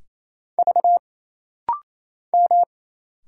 Morse code, heard as a keyed tone, about 27 wpm, 700 Hz. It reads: V M
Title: Single Letters - Rapid-Fire 28wpm